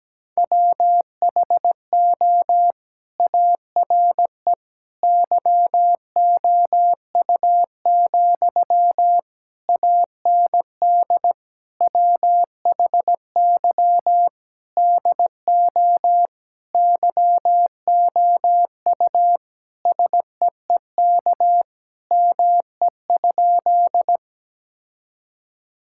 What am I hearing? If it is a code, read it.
Morse code, heard as a keyed tone, about 17 wpm, 700 Hz. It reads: WHO ARE YOU, AND WHY DO YOU SEEK ME?
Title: Sentences from Top 1600 Words 17wpm